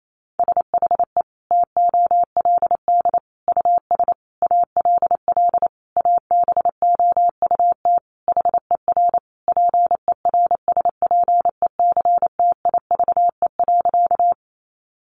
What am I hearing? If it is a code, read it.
Morse code, heard as a keyed tone, about 28 wpm, 700 Hz. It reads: SHE TOLD US ALL ABOUT HER PERSPECTIVE.
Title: Sentences from Top 1200 Words 28wpm